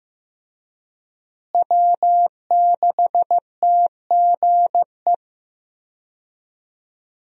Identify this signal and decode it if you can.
Morse code, heard as a keyed tone, about 15 wpm, 700 Hz. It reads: W6TGE
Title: USA Call Signs - 5 Characters - Repeated 3x - 15wpm